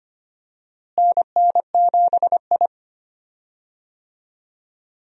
Morse code, heard as a keyed tone, about 25 wpm, 700 Hz. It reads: NN7I
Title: USA Call Signs - 4 Characters - Repeated 2x - 25wpm